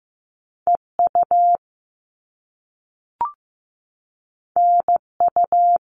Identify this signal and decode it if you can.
Morse code, heard as a keyed tone, about 15 wpm, 700 Hz. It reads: EU NU
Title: All Cut Numbers - 2-Digit 15wpm